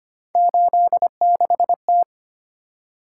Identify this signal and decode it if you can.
Morse code, heard as a keyed tone, about 25 wpm, 700 Hz. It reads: 86T
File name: Three Letter-Number Combinations - 25wpm